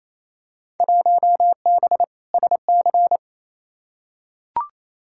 Morse code, heard as a keyed tone, about 28 wpm, 700 Hz. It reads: 1B SC
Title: Field Day Exchanges Encore - 28wpm